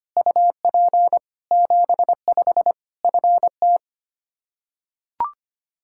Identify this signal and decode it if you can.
Morse code, heard as a keyed tone, about 25 wpm, 700 Hz. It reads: UP 75 FT
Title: Practice QSO - Exchange 2 - Info - 25wpm